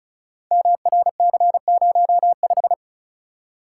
Morse code, heard as a keyed tone, about 36 wpm, 690 Hz. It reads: MRC05
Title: Code Groups - 5 Letter:Numbers - 35wpm